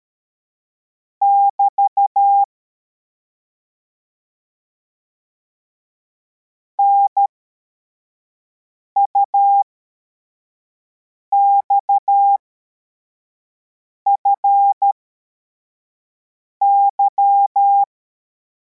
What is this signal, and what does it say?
Morse code, heard as a keyed tone, about 13 wpm, spaced as for 4 wpm, 800 Hz. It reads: <BT> NUXFY
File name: formation / cw / Text84